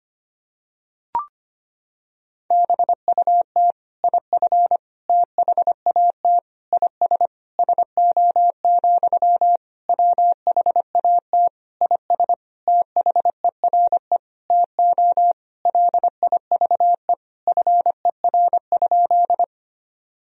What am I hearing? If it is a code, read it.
Morse code, heard as a keyed tone, about 25 wpm, 700 Hz. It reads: BUT IF THAT IS SO, WHAT IS THERE TO LIVE FER?
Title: Sentences from Top 200 Words and Abbreviations 25wpm